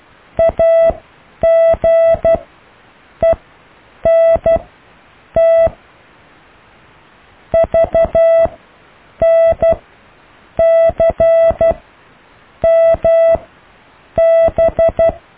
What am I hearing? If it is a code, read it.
Morse code, heard as a keyed tone, about 12 wpm, spaced as for 7 wpm, 660 Hz. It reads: AGENT VNCMB
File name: cw / Alle letters